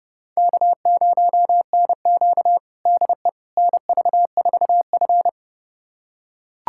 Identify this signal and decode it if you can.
Morse code, heard as a keyed tone, about 30 wpm, 700 Hz. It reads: K0NQ DE NV4F
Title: USA Callsign DE Callsign - 4 Characters - 30wpm